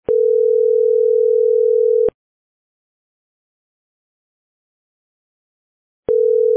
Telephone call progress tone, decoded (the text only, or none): ringback tone